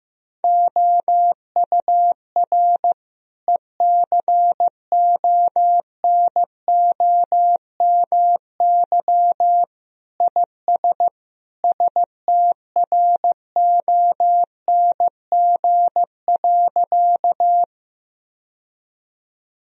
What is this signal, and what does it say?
Morse code, heard as a keyed tone, about 15 wpm, 700 Hz. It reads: OUR ECONOMY IS STRONG.
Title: Sentences from Top 1000 Words - Review Part 3 - 15wpm